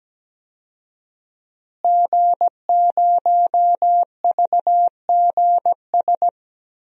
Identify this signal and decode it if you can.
Morse code, heard as a keyed tone, about 17 wpm, 700 Hz. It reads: G0VGS
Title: DXCC-100 Call Signs - 5 Characters - Repeated 3x - 17wpm